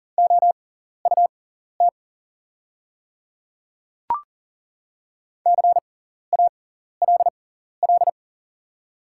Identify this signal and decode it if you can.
Morse code, heard as a keyed tone, about 40 wpm, spaced as for 14 wpm, 690 Hz. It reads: OUT CALL
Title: 6x Character Spacing - Top 200 Words - 40wpm